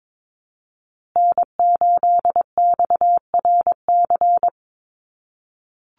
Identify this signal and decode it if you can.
Morse code, heard as a keyed tone, about 22 wpm, 700 Hz. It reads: N8XRC
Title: USA Call Signs Repeated 3x 22wpm